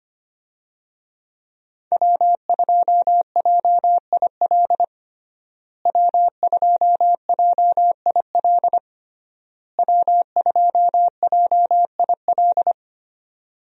Morse code, heard as a keyed tone, about 25 wpm, 700 Hz. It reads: W2JIL W2JIL W2JIL
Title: USA Call Signs - 5 Characters - Repeated 3x - 25wpm